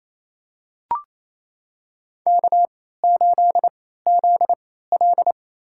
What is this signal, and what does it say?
Morse code, heard as a keyed tone, about 28 wpm, spaced as for 16 wpm, 700 Hz. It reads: K8ZL
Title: USA Call Signs - 4 Characters - 3x Character Spacing - 28wpm